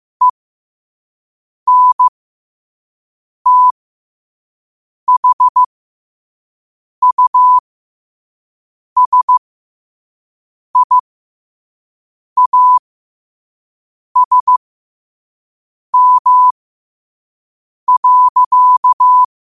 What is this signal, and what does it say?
Morse code, heard as a keyed tone, about 15 wpm, spaced as for 5 wpm, 1000 Hz. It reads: ENTHUSIASM.